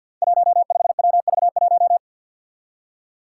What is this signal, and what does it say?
Morse code, heard as a keyed tone, about 50 wpm, 700 Hz. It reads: 15WV1
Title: Five Challenging Letter-Number Combinations 50wpm